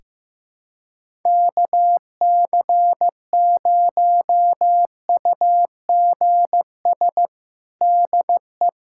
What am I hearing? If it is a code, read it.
Morse code, heard as a keyed tone, about 15 wpm, 700 Hz. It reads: KC0UGS DE
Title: USA Callsign DE Callsign - 6 Characters - 15wpm